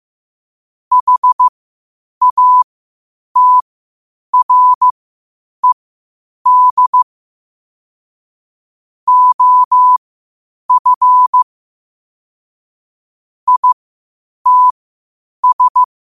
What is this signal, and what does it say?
Morse code, heard as a keyed tone, about 15 wpm, spaced as for 8 wpm, 1000 Hz. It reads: HATRED OF ITS
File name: qotdfiles / 8wpm